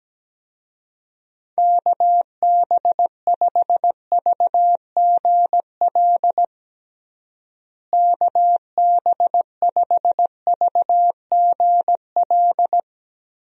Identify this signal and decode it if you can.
Morse code, heard as a keyed tone, about 17 wpm, 700 Hz. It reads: KB5VGL KB5VGL
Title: USA Call Signs - 6 Characters - Repeated 3x - 17wpm